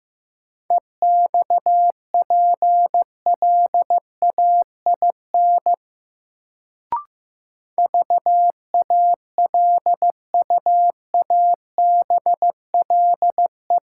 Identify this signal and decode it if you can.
Morse code, heard as a keyed tone, about 15 wpm, 700 Hz. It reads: EXPLAIN VALUABLE
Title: Sentences from Top 1100 Words - New Word Drill - 15wpm